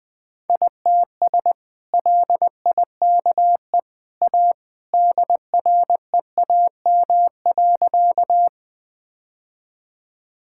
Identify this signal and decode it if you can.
Morse code, heard as a keyed tone, about 20 wpm, 700 Hz. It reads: ITS LIKE A DREAM.
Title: Sentences with 2-Word Phrases - Spoken Beginning - 20wpm